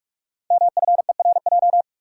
Morse code, heard as a keyed tone, about 44 wpm, 700 Hz. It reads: MFERJ
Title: Code Groups - 5 Letters - 45wpm